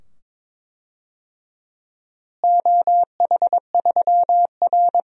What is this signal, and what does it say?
Morse code, heard as a keyed tone, about 22 wpm, 700 Hz. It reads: OH3R
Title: DXCC-100 Call Signs - 4 Characters - Repeated 2x - 22wpm